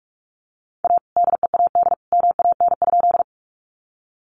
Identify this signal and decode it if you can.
Morse code, heard as a keyed tone, about 46 wpm, 700 Hz. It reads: A DEAD MAN?
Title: Sentences from Top 1500 Words 45wpm